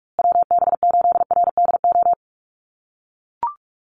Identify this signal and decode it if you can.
Morse code, heard as a keyed tone, about 44 wpm, 700 Hz. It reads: WB8RDO
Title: USA Call Signs Repeated 2x 45wpm